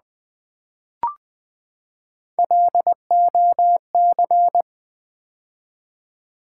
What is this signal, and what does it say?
Morse code, heard as a keyed tone, about 20 wpm, 700 Hz. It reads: LOC
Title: Build A Word - 3 Letters At A Time - 8-Letter Words - 20wpm